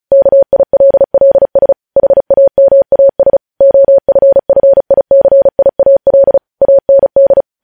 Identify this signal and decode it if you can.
Morse code, heard as a keyed tone, about 36 wpm, 550 Hz. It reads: KILLS HAMAS OFFICIAL AND
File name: podcast / News headlines-1x-35x35